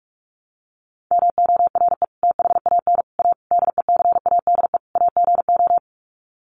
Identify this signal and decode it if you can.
Morse code, heard as a keyed tone, about 44 wpm, 700 Hz. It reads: MORE THAN A DECADE AGO
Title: Sets of 5 Words Encore2 45wpm